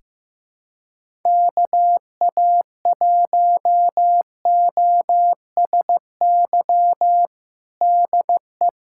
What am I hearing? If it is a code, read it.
Morse code, heard as a keyed tone, about 16 wpm, 700 Hz. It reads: KA1OSY DE
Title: USA Callsign DE Callsign - 6 Characters - 15wpm